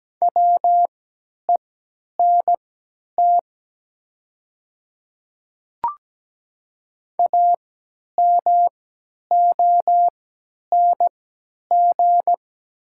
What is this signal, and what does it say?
Morse code, heard as a keyed tone, about 17 wpm, spaced as for 10 wpm, 690 Hz. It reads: WENT AMONG
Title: 3x Character Spacing - Top 500 Words - 17wpm